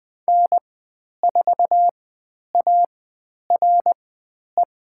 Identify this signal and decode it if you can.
Morse code, heard as a keyed tone, about 20 wpm, spaced as for 10 wpm, 700 Hz. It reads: N4ARE
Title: 095 Lesson 16g - Letter c - Call Signs 20wpm